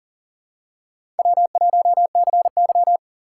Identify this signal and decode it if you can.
Morse code, heard as a keyed tone, about 40 wpm, 700 Hz. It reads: W1CY
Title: USA Call Signs - 4 Characters - 4x Character Spacing - 40wpm